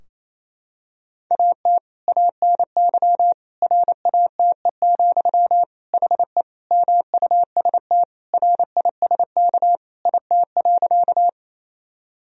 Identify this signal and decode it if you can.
Morse code, heard as a keyed tone, about 28 wpm, 700 Hz. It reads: AT ANY RATE, HE MUST RISK IT.
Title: Sentences from Top 1000 Words - Review Part 4 - 28wpm